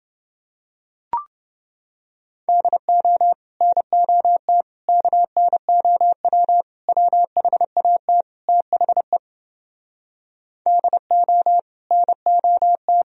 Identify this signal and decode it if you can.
Morse code, heard as a keyed tone, about 30 wpm, 700 Hz. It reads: DO NOT KNOW WHAT THE DO NOT